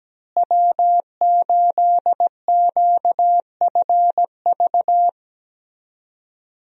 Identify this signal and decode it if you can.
Morse code, heard as a keyed tone, about 17 wpm, 700 Hz. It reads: W8QFV
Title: USA Call Signs - 5 Characters - Repeated 3x - 17wpm